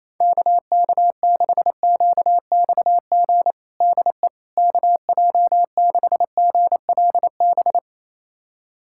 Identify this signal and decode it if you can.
Morse code, heard as a keyed tone, about 28 wpm, 700 Hz. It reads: KK6QXG DE KJ6GLB
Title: USA Callsign DE Callsign - 6 Characters - 28wpm